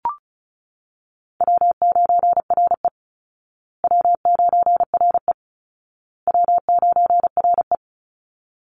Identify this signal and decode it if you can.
Morse code, heard as a keyed tone, about 34 wpm, 700 Hz. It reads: W9RE W9RE W9RE
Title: Call Signs Repeated 2x - 35wpm